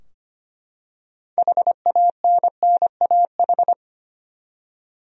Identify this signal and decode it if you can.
Morse code, heard as a keyed tone, about 25 wpm, 700 Hz. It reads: HANNAH
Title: USA Common Names - 25wpm